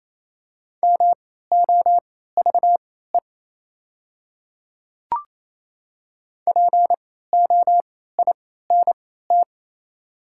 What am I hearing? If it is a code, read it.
Morse code, heard as a keyed tone, about 28 wpm, spaced as for 16 wpm, 700 Hz. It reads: MOVE POINT
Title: 3x Character Spacing - Top 500 Words - 28wpm